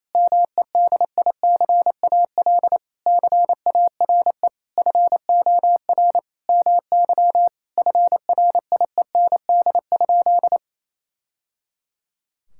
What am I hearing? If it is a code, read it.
Morse code, heard as a keyed tone, about 28 wpm, 700 Hz. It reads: MEDICAL CARE FOR MY FRIEND?
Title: Sentences from Top 1000 Words - Review Part 5 - 28wpm